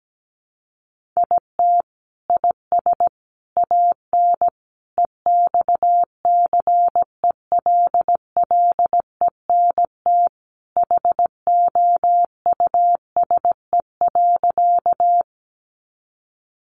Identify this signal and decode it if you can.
Morse code, heard as a keyed tone, about 17 wpm, 700 Hz. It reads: IT IS AN EXCELLENT HOUSE.
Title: Sentences from Top 1000 Words 17wpm